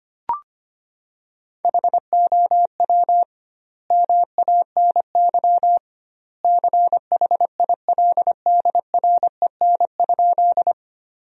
Morse code, heard as a keyed tone, about 25 wpm, 700 Hz. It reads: HOW MANY CHILDREN?
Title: 2x Word-Spacing - Sentences from Top 300 Words - 25wpm